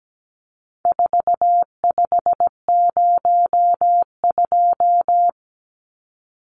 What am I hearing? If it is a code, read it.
Morse code, heard as a keyed tone, about 17 wpm, 700 Hz. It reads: 4502
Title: Numbers - 4-Digit 17wpm